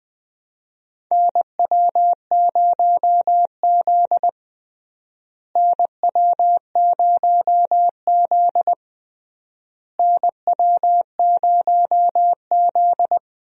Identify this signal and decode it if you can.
Morse code, heard as a keyed tone, about 20 wpm, 700 Hz. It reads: NW0Z NW0Z NW0Z
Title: USA Call Signs - 4 Characters - Repeated 3x - 20wpm